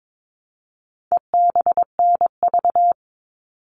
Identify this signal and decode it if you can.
Morse code, heard as a keyed tone, about 22 wpm, 700 Hz. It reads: EBNV
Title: All Cut Numbers - 4-Digit 22wpm